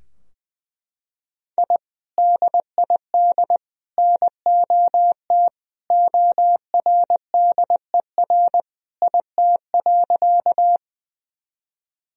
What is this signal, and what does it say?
Morse code, heard as a keyed tone, about 21 wpm, 700 Hz. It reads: I DID NOT ORDER IT.
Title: Sentences from Top 1000 Words - Review Part 2 - 20wpm